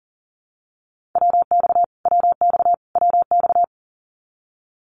Morse code, heard as a keyed tone, about 40 wpm, 700 Hz. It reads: WX WX WX
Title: Abbreviations Repeated 3x - 40wpm